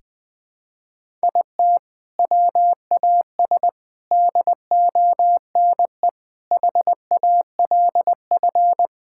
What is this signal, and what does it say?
Morse code, heard as a keyed tone, about 21 wpm, 700 Hz. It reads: IT WAS DONE HALF
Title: Sentences from Top 1000 Words - Review Part 5 - 20wpm